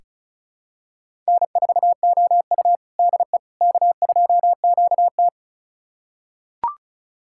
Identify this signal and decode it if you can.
Morse code, heard as a keyed tone, about 36 wpm, 700 Hz. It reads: N4OU DE K2QT
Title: USA Callsign DE Callsign - 4 Characters - 35wpm